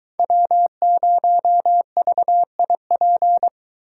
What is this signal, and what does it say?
Morse code, heard as a keyed tone, about 22 wpm, 700 Hz. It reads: W0VIP